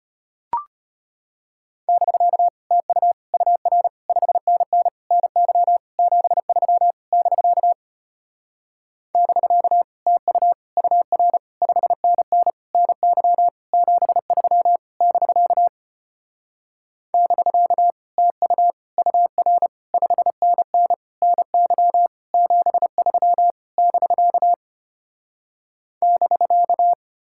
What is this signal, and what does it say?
Morse code, heard as a keyed tone, about 33 wpm, 700 Hz. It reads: <BK> TU UR 5NN NY 73 <BK> <BK> TU UR 5NN NY 73 <BK> <BK> TU UR 5NN NY 73 <BK> <BK>